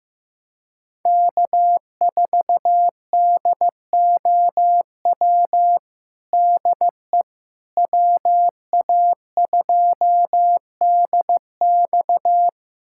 Morse code, heard as a keyed tone, about 16 wpm, 700 Hz. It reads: K4DOW DE WA2DX
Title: USA Callsign DE Callsign - 5 Characters - 15wpm